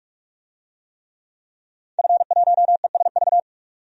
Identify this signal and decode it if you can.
Morse code, heard as a keyed tone, about 46 wpm, 700 Hz. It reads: F1ESV
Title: DXCC-100 Call Signs - 5 Characters - 45wpm